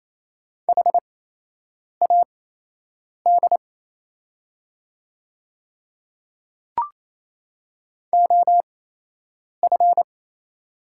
Morse code, heard as a keyed tone, about 28 wpm, spaced as for 8 wpm, 700 Hz. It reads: HAD OF
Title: 8x Character Spacing - Top 100 Words - 28wpm